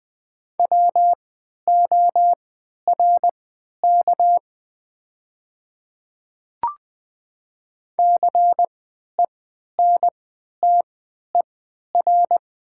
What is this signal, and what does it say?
Morse code, heard as a keyed tone, about 20 wpm, spaced as for 11 wpm, 700 Hz. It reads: WORK CENTER
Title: 3x Character Spacing - Top 500 Words - 20wpm